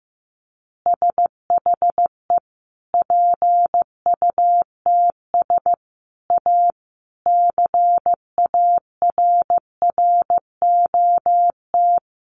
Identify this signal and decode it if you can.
Morse code, heard as a keyed tone, about 14 wpm, 700 Hz. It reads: SHE PUTS A CARROT
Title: Easy Phrases 15wpm